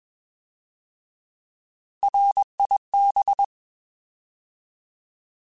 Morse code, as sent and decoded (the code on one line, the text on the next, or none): .-. .. -...
RIB